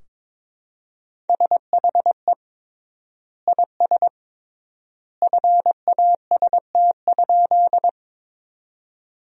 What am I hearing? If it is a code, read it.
Morse code, heard as a keyed tone, about 22 wpm, 700 Hz. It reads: SHE IS FAST?
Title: 3x Word-Spacing - Sentences from Top 400 Words - 22wpm